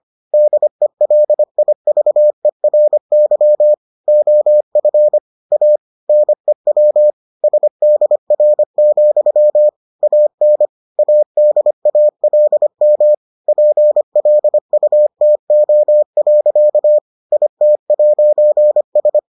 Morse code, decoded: DELIVERY OF A NEW SDR, AN ADALM PLUTO. IT'S

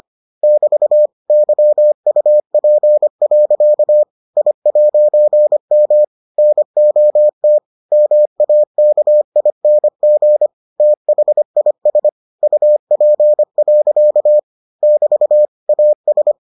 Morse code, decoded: <BT> YUP. I'M NOT MAKING THIS UP. <BT> AS

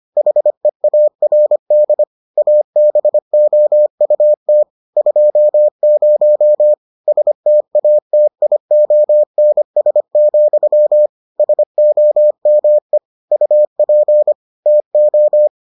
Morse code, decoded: HEARD ABOUT 20 STATIONS, SOME UP TO

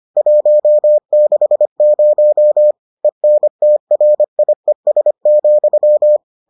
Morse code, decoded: 160 ENTRIES,

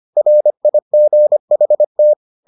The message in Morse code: RIGHT

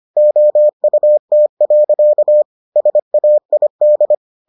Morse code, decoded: OUT. SAID

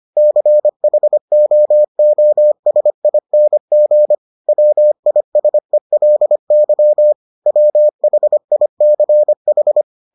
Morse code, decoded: CHOOSING WISELY WHICH

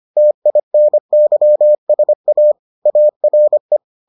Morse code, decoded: TINYSA ARE

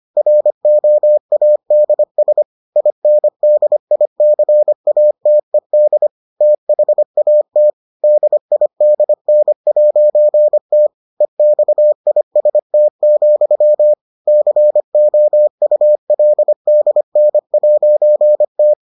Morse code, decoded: ROADS INDICATED THAT DIDN'T EXIST, COULDN'T

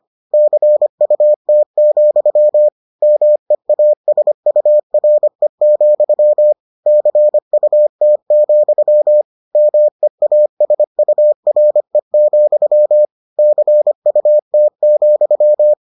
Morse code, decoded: CUT, MEASURE, CUT, MEASURE, CUT,

